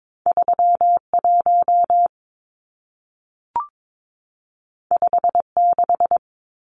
Morse code, heard as a keyed tone, about 22 wpm, 700 Hz. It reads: 31 56
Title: Numbers - 2-Digit 22wpm